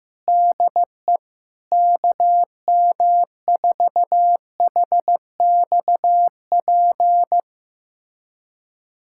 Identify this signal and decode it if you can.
Morse code, heard as a keyed tone, about 15 wpm, 700 Hz. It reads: DE KM4HXP
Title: USA Callsign DE Callsign 15wpm